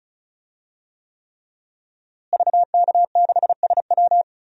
Morse code, heard as a keyed tone, about 36 wpm, 700 Hz. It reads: VK6SW
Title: DXCC-100 Call Signs - 5 Characters - Repeated 3x - 35wpm